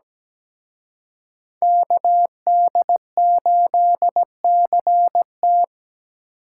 Morse code, heard as a keyed tone, about 19 wpm, 700 Hz. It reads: KD8CT
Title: Speed Racing - USA 2x2 Call Signs 17wpm